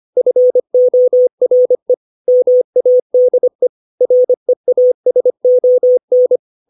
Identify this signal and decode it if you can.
Morse code, heard as a keyed tone, about 25 wpm, 500 Hz. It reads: FORE MADE REASON
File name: 632 words / 632 Words 25WPM